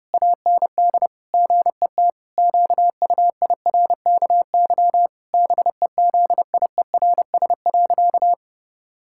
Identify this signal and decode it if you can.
Morse code, heard as a keyed tone, about 30 wpm, 700 Hz. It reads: AND GET QUIRKY BEZIERS.
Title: Pangram Challenge v2 - 30wpm